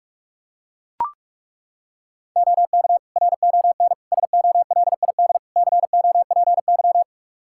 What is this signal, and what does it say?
Morse code, heard as a keyed tone, about 43 wpm, 700 Hz. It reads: OK RON SOLID COPY
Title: Practice QSO - Exchange 3 - More Info - 45wpm